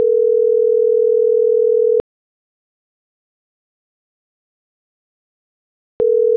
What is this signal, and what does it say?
Ringback tone (North America), 440 + 480 Hz.